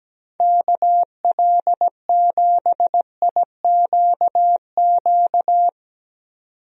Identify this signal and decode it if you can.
Morse code, heard as a keyed tone, about 17 wpm, 700 Hz. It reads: KL7IQQ
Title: DXCC-100 Call Signs - 6 Characters - 17wpm